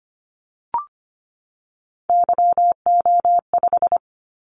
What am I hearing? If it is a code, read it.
Morse code, heard as a keyed tone, about 25 wpm, 700 Hz. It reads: YO5
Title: Callsign Prefixes for Top 25 DXCC - 25wpm